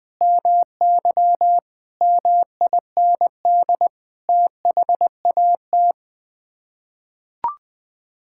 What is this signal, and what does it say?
Morse code, heard as a keyed tone, about 20 wpm, 700 Hz. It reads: MY MIND THAT
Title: Sets of 8 Words - 20wpm